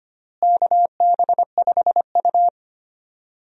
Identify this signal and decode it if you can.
Morse code, heard as a keyed tone, about 25 wpm, 700 Hz. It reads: KB5U
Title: USA Call Signs - 4 Characters - Repeated 3x - 25wpm